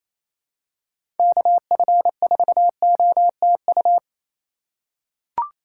Morse code, heard as a keyed tone, about 28 wpm, 700 Hz. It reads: KF4OTU
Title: USA Call Signs - 6 Characters - 2x Character Spacing - 28wpm